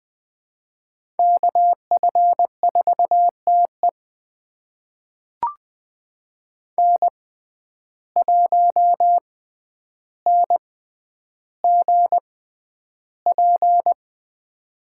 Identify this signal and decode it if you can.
Morse code, heard as a keyed tone, about 20 wpm, spaced as for 10 wpm, 700 Hz. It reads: KF4TE N1NGP
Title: USA Call Signs - 5 Characters - 6x Character Spacing - 20wpm